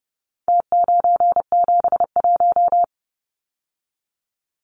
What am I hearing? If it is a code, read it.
Morse code, heard as a keyed tone, about 30 wpm, 700 Hz. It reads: T971
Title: Numbers - Contest Style - 4-Digit 30wpm